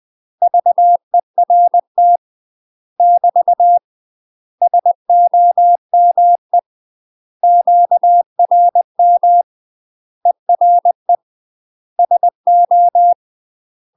Morse code, heard as a keyed tone, about 20 wpm, 700 Hz. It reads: VERT <BT> SOME QRM ERE SO
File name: UK MASTER QSOs TRAINING / 20WPM / UK MASTER QSOs 0029